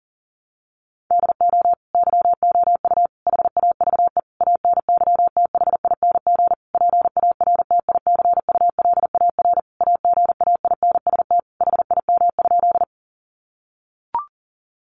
Morse code, heard as a keyed tone, about 40 wpm, 700 Hz. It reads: DO YOU HAVE ANYTHING PARTICULAR AGAINST HIM?
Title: Sentences from Top 1000 Words 40wpm